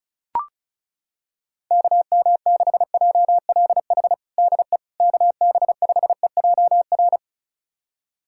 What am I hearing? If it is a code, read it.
Morse code, heard as a keyed tone, about 34 wpm, 700 Hz. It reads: KM6JLH DE KB5EJR
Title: USA Callsign DE Callsign - 6 Characters - 35wpm